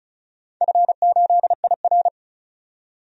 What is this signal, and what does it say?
Morse code, heard as a keyed tone, about 36 wpm, 700 Hz. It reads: F8IR
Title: DXCC-100 Call Signs - 4 Characters - 35wpm